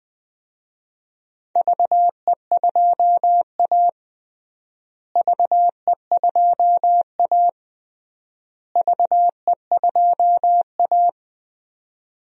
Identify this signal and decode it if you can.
Morse code, heard as a keyed tone, about 20 wpm, 700 Hz. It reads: VE2A VE2A VE2A
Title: DXCC-100 Call Signs - 4 Characters - Repeated 3x - 20wpm